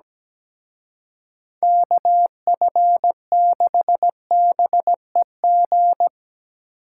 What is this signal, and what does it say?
Morse code, heard as a keyed tone, about 19 wpm, 700 Hz. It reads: KF6BEG
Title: Speed Racing - USA 2x3 Call Signs 17wpm